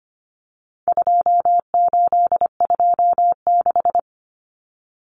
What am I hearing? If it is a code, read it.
Morse code, heard as a keyed tone, about 25 wpm, 700 Hz. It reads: 2826
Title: Numbers - 4-Digit 25wpm